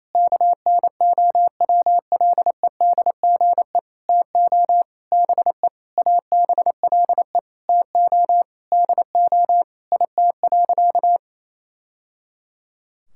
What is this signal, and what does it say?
Morse code, heard as a keyed tone, about 28 wpm, 700 Hz. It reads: KNOWLEDGE TO BE ABLE TO DO IT.
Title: Sentences from Top 1000 Words - Review Part 6 - 28wpm